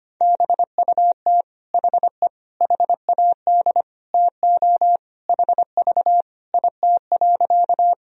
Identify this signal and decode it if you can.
Morse code, heard as a keyed tone, about 25 wpm, 700 Hz. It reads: BUT HE HAD TO HV IT.
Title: Sentences from Top 100 Words and Abbreviations 25wpm